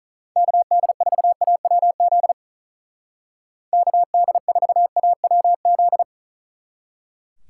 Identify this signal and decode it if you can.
Morse code, heard as a keyed tone, about 44 wpm, 700 Hz. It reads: KD4AWZ KD4AWZ